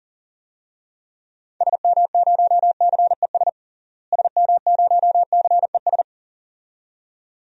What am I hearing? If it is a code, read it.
Morse code, heard as a keyed tone, about 40 wpm, 700 Hz. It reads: SM0CES SM0CES
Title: DXCC-100 Call Signs - 6 Characters - Repeated 2x - 40wpm